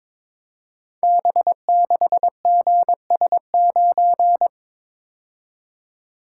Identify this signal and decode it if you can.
Morse code, heard as a keyed tone, about 22 wpm, 690 Hz. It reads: B6GS9
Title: Code Groups - 5 Letter:Numbers - 22wpm